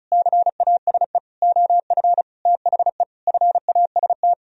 Morse code, heard as a keyed tone, about 34 wpm, 700 Hz. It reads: CASE OF THE FAST
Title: Sentences from Top 1000 Words - Review Part 2 - 35wpm